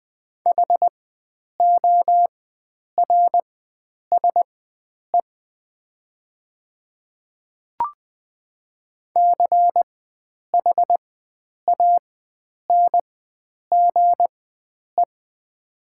Morse code, heard as a keyed tone, about 20 wpm, spaced as for 9 wpm, 700 Hz. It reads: HORSE CHANGE
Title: 4x Character Spacing - Top 500 Words - 20wpm